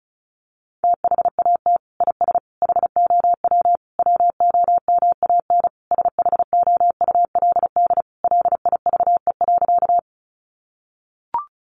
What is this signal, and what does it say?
Morse code, heard as a keyed tone, about 34 wpm, 700 Hz. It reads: THAT IS HOW WOMAN SHOULD LIVE.
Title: Sentences from Top 1100 Words 35wpm